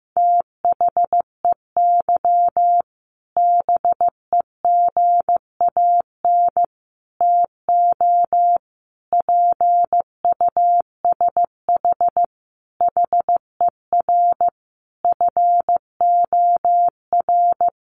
Morse code, heard as a keyed tone, about 15 wpm, 700 Hz. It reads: THEY BEGAN TO PUSH HER FOR
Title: Sentences from Top 1300 Words 15wpm